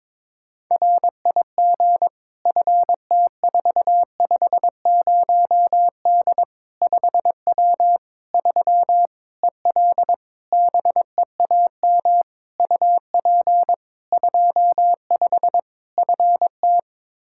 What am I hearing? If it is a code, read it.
Morse code, heard as a keyed tone, about 22 wpm, 700 Hz. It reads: RIG FT450D 5W 3 EL BEAM UP 25 FT
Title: Practice QSO - Exchange 2 - Info - 22wpm